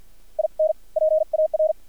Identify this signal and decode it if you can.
Morse code, heard as a keyed tone, about 50 wpm, spaced as for 42 wpm, 630 Hz. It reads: IN PAR
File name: archived / CRSnet